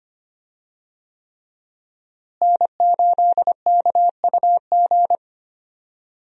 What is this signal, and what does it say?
Morse code, heard as a keyed tone, about 25 wpm, 700 Hz. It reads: N8KUG
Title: USA Call Signs - 5 Characters - Repeated 3x - 25wpm